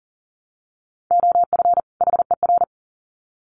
Morse code, heard as a keyed tone, about 40 wpm, 700 Hz. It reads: OF HER
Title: Sets of 2 Words 40wpm